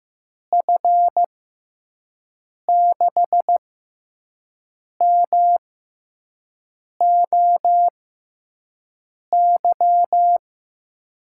Morse code, heard as a keyed tone, about 15 wpm, spaced as for 5 wpm, 700 Hz. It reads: F6MOY
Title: USA Call Signs - 6 Characters - 6x Character Spacing - 15wpm